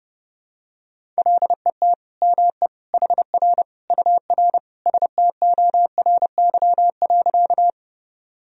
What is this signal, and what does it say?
Morse code, heard as a keyed tone, about 30 wpm, 700 Hz. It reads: LET ME HR UR STORY.
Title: Sentences from Top 300 Words and Abbreviations 30wpm